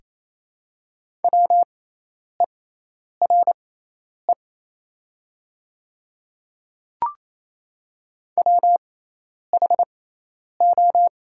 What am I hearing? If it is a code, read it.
Morse code, heard as a keyed tone, about 27 wpm, spaced as for 10 wpm, 690 Hz. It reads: WERE WHO
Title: 6x Character Spacing - Top 100 Words - 28wpm